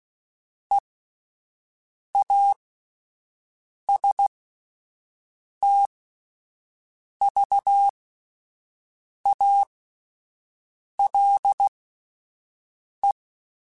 Morse code, heard as a keyed tone, about 16 wpm, spaced as for 5 wpm, 800 Hz. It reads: EASTVALE
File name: fromSMs / FromSMs mp3s / Cities@5